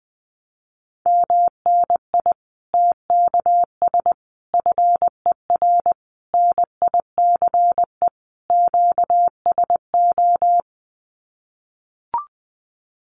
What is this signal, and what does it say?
Morse code, heard as a keyed tone, about 20 wpm, 700 Hz. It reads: MNI TKS FER NICE QSO
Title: Common QSO Phrases 20wpm